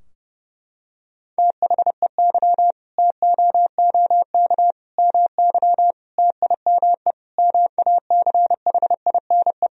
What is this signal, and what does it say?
Morse code, heard as a keyed tone, about 30 wpm, 700 Hz. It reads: THEY TOOK MY TIME MACHINE
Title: Sentences from Top 1000 Words - Review Part 2 - 30wpm